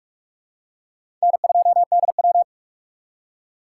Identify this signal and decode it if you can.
Morse code, heard as a keyed tone, about 42 wpm, 700 Hz. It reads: N2DW